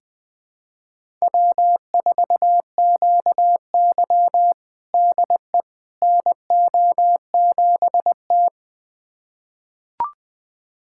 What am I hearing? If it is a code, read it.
Morse code, heard as a keyed tone, about 20 wpm, 700 Hz. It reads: W4QY DE NO7T
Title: USA Callsign DE Callsign - 4 Characters - 20wpm